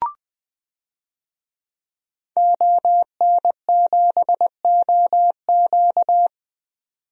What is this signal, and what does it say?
Morse code, heard as a keyed tone, about 20 wpm, 700 Hz. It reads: ON7OQ
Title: DXCC-100 Call Signs - 5 Characters - 20wpm